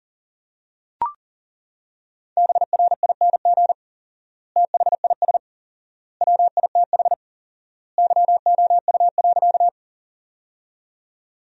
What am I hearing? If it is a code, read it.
Morse code, heard as a keyed tone, about 40 wpm, 700 Hz. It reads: BRING THIS WITH YOU.
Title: 4x Word-Spacing - Sentences from Top 500 Words - 40wpm